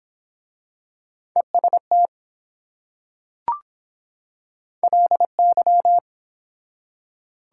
Morse code, heard as a keyed tone, about 22 wpm, 700 Hz. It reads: EST LY